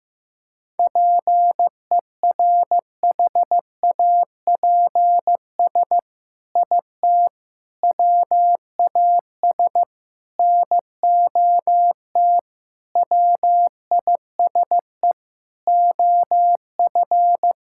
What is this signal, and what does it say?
Morse code, heard as a keyed tone, about 15 wpm, 700 Hz. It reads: PERHAPS IT WAS NOT WISE OF
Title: Sentences from Top 1600 Words 15wpm